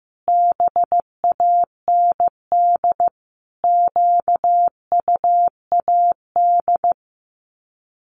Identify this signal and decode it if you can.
Morse code, heard as a keyed tone, about 15 wpm, 700 Hz. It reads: BAND QUAD
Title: Common QSO Phrases 15wpm